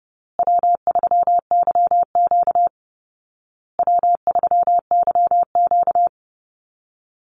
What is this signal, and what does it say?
Morse code, heard as a keyed tone, about 28 wpm, 700 Hz. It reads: W3YQ W3YQ
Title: Easy Call Signs Repeated 3x - 30wpm